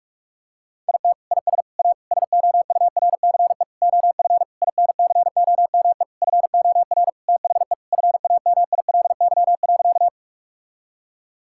Morse code, heard as a keyed tone, about 46 wpm, 700 Hz. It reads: IT IS A SOURCE OF INCOME FOR THE FAMILY.
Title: Sentences from Top 1000 Words - Review Part 5 - 45wpm